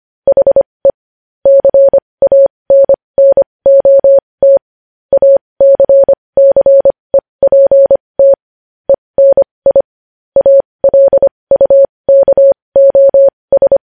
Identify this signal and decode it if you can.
Morse code, heard as a keyed tone, about 25 wpm, spaced as for 20 wpm, 550 Hz. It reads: HE CANNOT ACCEPT ENI ALUKOS
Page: news headlines in morse code at 20 wpm